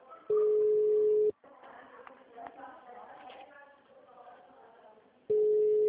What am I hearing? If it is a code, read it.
Ringback tone (Europe, 425 Hz), with speech.
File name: out- -2024- - - .464